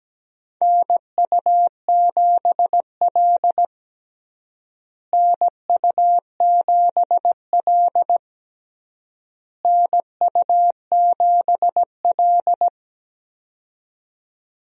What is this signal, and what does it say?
Morse code, heard as a keyed tone, about 17 wpm, 700 Hz. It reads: NU7L NU7L NU7L
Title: USA Call Signs - 4 Characters - Repeated 3x - 17wpm